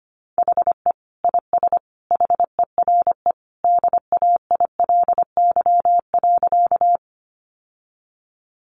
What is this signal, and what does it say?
Morse code, heard as a keyed tone, about 25 wpm, 700 Hz. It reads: HE IS HERE DAILY.
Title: Sentences from Top 1100 Words 25wpm